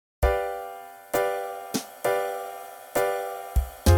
There is no code.